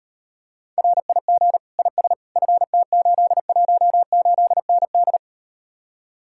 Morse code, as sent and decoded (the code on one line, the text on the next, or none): .-. .. --. / .. ... / ..-. - ---.. .---- ---.. -. -..
RIG IS FT818ND